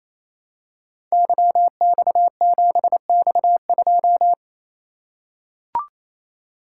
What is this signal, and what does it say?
Morse code, heard as a keyed tone, about 29 wpm, 700 Hz. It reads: YX7X2
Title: Code Groups - 5 Letter:Numbers - 28wpm